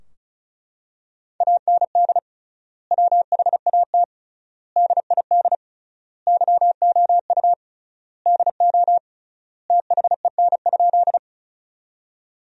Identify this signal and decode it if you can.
Morse code, heard as a keyed tone, about 36 wpm, 700 Hz. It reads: AND WHAT DID YOU DO THEN?
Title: 3x Word-Spacing - Sentences from Top 100 Words - 35wpm